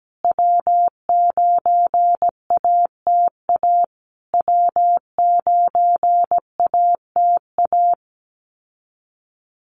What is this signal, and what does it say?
Morse code, heard as a keyed tone, about 17 wpm, 700 Hz. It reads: W9ATA W9ATA
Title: USA Call Signs Repeated 3x 17wpm